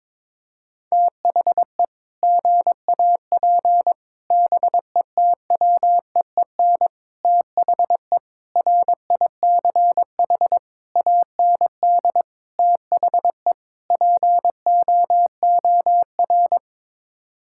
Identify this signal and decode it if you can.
Morse code, heard as a keyed tone, about 21 wpm, 700 Hz. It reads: THE GAP BETWEEN THE RICH AND THE POOR
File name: practice / 22wpm / Sets of 8 Words - 22wpm